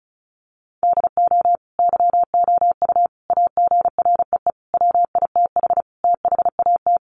Morse code, heard as a keyed tone, about 34 wpm, 700 Hz. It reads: DO YOU AGREE WITH THAT
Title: Sets of 5 Words Encore 35wpm